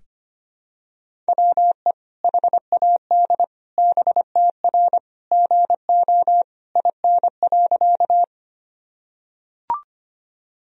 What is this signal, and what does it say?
Morse code, heard as a keyed tone, about 25 wpm, 700 Hz. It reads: WE HAD BTR GO IN.
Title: Sentences from Top 400 Words and Abbreviations 25wpm